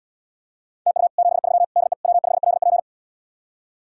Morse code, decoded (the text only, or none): TU 73 DE W5LP